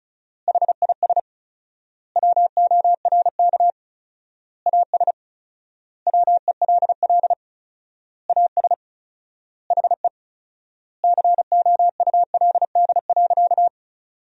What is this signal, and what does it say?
Morse code, heard as a keyed tone, about 34 wpm, 700 Hz. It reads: HIS WORK AS WELL AS HE COULD.
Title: 4x Word-Spacing - Sentences from Top 500 Words - 35wpm